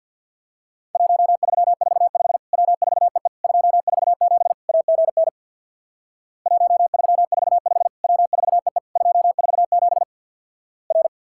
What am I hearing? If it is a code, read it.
Morse code, heard as a keyed tone, about 49 wpm, 700 Hz. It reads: 1345 W4EE 247 AGN 1345 W4EE 247 R
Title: Worked All Europe DX Contest - QTC Exchange with Repeats - 50wpm